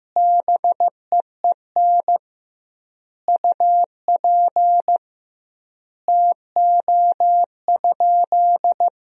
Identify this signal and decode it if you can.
Morse code, heard as a keyed tone, about 15 wpm, 700 Hz. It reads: BEEN UP TO?
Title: 2x Word-Spacing - Sentences from Top 100 Words - 15wpm